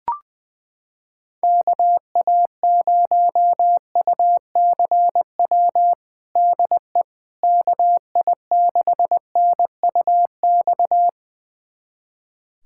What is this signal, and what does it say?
Morse code, heard as a keyed tone, about 20 wpm, 700 Hz. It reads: KA0UCW DE KI6NUX
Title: USA Callsign DE Callsign - 6 Characters - 20wpm